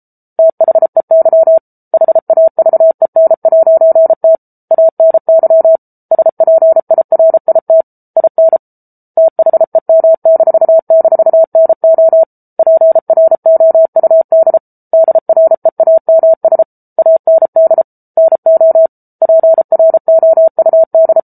Morse code, heard as a keyed tone, about 34 wpm, 650 Hz. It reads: THEY HAVEN'T ANY SPIRIT IN THEM--NO PROUD DREAMS AND NO PROUD